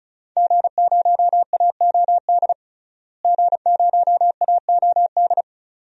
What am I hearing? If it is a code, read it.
Morse code, heard as a keyed tone, about 36 wpm, 700 Hz. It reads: G0AOD G0AOD
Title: DXCC-100 Call Signs - 5 Characters - Repeated 2x - 35wpm